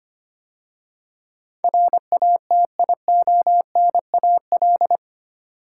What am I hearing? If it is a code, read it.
Morse code, heard as a keyed tone, about 25 wpm, 690 Hz. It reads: RATIONAL
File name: 8 Letter Words - First 3 Letters Given - 25wpm